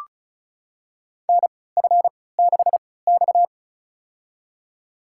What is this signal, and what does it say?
Morse code, heard as a keyed tone, about 36 wpm, spaced as for 20 wpm, 700 Hz. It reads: NF6X
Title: USA Call Signs - 3x Character Spacing - 35wpm